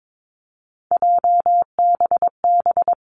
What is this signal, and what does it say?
Morse code, heard as a keyed tone, about 22 wpm, 700 Hz. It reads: JBB
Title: Three Challenging Letter-Number Combinations 22wpm